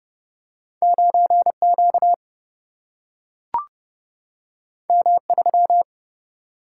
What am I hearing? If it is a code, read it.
Morse code, heard as a keyed tone, about 30 wpm, 700 Hz. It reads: 9 Q M 3
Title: Single-Letter Single-Number 30wpm